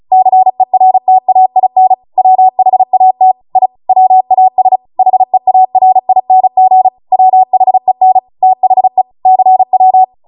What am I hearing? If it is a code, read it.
Morse code, heard as a keyed tone, about 36 wpm, 750 Hz. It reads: CERTAIN WHAT I WAS HEARING WHEN THE CW